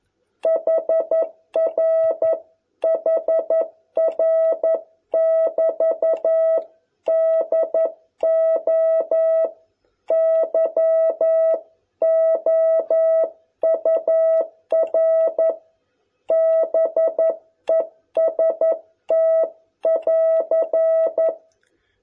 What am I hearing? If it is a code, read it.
Morse code, heard as a keyed tone, about 11 wpm, 660 Hz. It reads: HR HR <BT> DO YOUR BEST <AR>